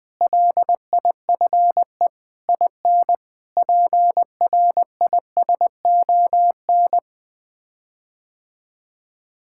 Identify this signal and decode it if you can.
Morse code, heard as a keyed tone, about 20 wpm, 700 Hz. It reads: LIFE IN PRISON
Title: Sets of 8 Words - 20wpm